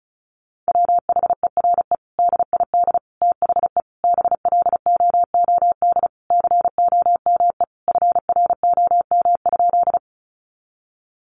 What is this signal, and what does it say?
Morse code, heard as a keyed tone, about 34 wpm, 700 Hz. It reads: WHERE DID THE BLOOD COME FROM?
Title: Sentences from Top 1100 Words 35wpm